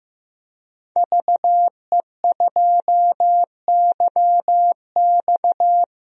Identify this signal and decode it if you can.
Morse code, heard as a keyed tone, about 14 wpm, 700 Hz. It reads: VE2YX
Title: DXCC-100 Call Signs - 5 Characters - Repeated 2x - 15wpm